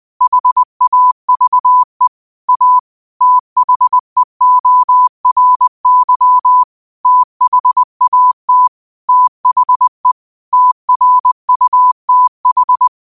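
Morse code, heard as a keyed tone, about 20 wpm, 1000 Hz. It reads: HAVE A THEORY THAT THE TRUTH
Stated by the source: Quote of the day sent in Morse at 20 words per minute.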